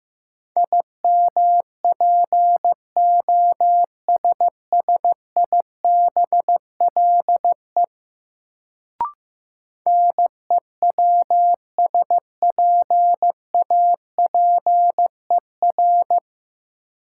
Morse code, heard as a keyed tone, about 15 wpm, 700 Hz. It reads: IMPOSSIBLE NEWSPAPER
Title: Sentences from Top 1100 Words - New Word Drill Alt - 15wpm